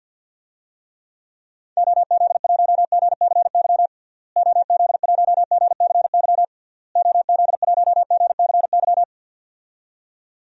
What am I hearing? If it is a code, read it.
Morse code, heard as a keyed tone, about 50 wpm, 700 Hz. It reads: OZ1GCY OZ1GCY OZ1GCY
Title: DXCC-100 Call Signs - 6 Characters - Repeated 3x - 50wpm